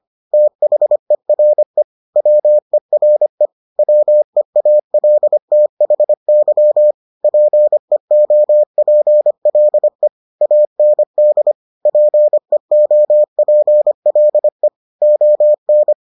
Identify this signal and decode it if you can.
Morse code, heard as a keyed tone, about 25 wpm, 600 Hz. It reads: THERE WERE WEALTHY PEOPLE AND PEOPLE ON